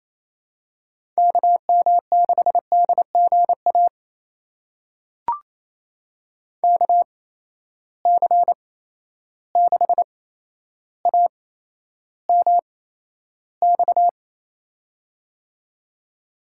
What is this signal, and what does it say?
Morse code, heard as a keyed tone, about 28 wpm, spaced as for 12 wpm, 700 Hz. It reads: KM6DGA KC6AMX
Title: USA Call Signs - 6 Characters - 8x Character Spacing - 28wpm